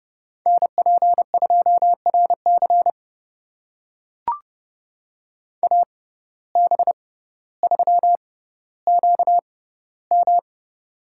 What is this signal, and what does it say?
Morse code, heard as a keyed tone, about 30 wpm, spaced as for 15 wpm, 700 Hz. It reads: NP2RC AB3QM
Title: USA Call Signs - 5 Characters - 6x Character Spacing - 30wpm